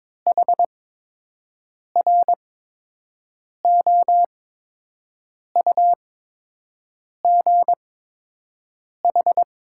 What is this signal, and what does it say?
Morse code, heard as a keyed tone, about 22 wpm, spaced as for 6 wpm, 700 Hz. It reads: HROUGH
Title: 8x Character Spacing - Top 300 Words - 22wpm